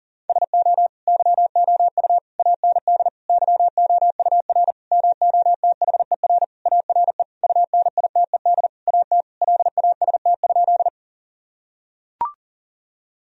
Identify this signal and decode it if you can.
Morse code, heard as a keyed tone, about 40 wpm, 700 Hz. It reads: SO YOU AND YOUR MOTHER ARE UNITED AT LAST?
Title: Sentences from Top 1000 Words - Review Part 4 - 40wpm